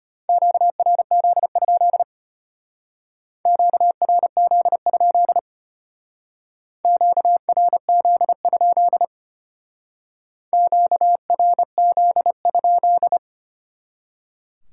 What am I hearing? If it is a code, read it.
Morse code, heard as a keyed tone, about 32 wpm, 700 Hz. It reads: QRZ? QRZ? QRZ? QRZ?